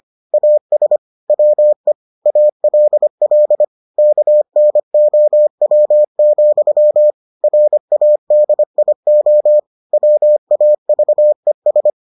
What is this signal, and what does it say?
Morse code, heard as a keyed tone, about 25 wpm, 600 Hz. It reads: AS WE ALL KNOW, RADIO WAVES